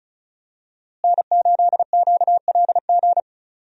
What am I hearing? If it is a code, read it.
Morse code, heard as a keyed tone, about 34 wpm, 700 Hz. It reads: N8QLG
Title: USA Call Signs - 5 Characters - Repeated 2x - 35wpm